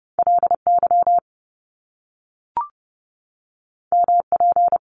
Morse code, heard as a keyed tone, about 30 wpm, 700 Hz. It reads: LY MP
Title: Top Two Letter Combinations in Words 30wpm